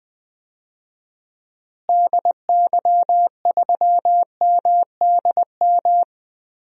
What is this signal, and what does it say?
Morse code, heard as a keyed tone, about 20 wpm, 700 Hz. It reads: DY3MDM
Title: DXCC-100 Call Signs - 6 Characters - Repeated 2x - 20wpm